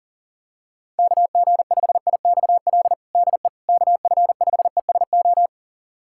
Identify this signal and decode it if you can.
Morse code, heard as a keyed tone, about 40 wpm, 700 Hz. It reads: KG5IXL DE KF5ESO
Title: USA Callsign DE Callsign - 6 Characters - 40wpm